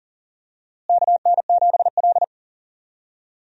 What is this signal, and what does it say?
Morse code, heard as a keyed tone, about 37 wpm, 700 Hz. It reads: KN7L
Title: USA Call Signs - 4 Characters - Repeated 3x - 40wpm